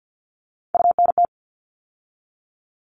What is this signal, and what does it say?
Morse code, heard as a keyed tone, about 50 wpm, 700 Hz. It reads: UNT
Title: All Cut Numbers - 3-Digit 50wpm